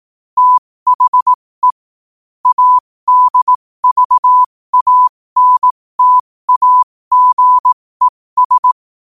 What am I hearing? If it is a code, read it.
Morse code, heard as a keyed tone, about 18 wpm, spaced as for 15 wpm, 1000 Hz. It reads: THE ADVANTAGES
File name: qotdfiles / 15wpm